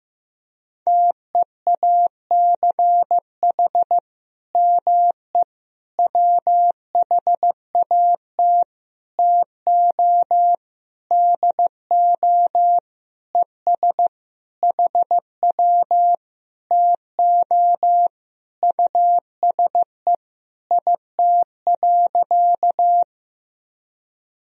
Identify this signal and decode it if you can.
Morse code, heard as a keyed tone, about 15 wpm, 700 Hz. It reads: TEACH ME WHAT TO DO ES HW TO USE IT.
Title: Sentences from Top 500 Words and Abbreviations 15wpm